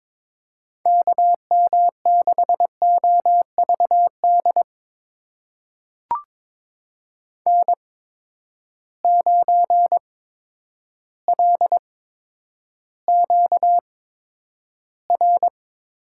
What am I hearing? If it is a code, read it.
Morse code, heard as a keyed tone, about 22 wpm, spaced as for 10 wpm, 700 Hz. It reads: KM6OVD N9LQR
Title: USA Call Signs - 8x Character Spacing - 22wpm